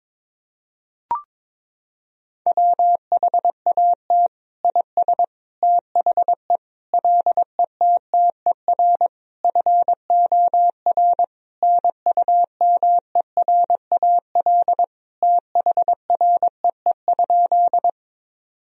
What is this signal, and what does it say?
Morse code, heard as a keyed tone, about 22 wpm, 700 Hz. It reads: WHAT IS THE LETTER FOR NUMERAL THREE?
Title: Sentences from Top 1000 Words - Review Part 2 - 22wpm